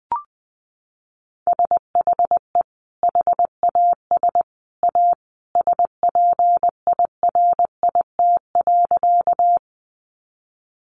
Morse code, heard as a keyed tone, about 20 wpm, 700 Hz. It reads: SHE HAS A SPIRIT.
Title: Sentences from Top 1400 Words 20wpm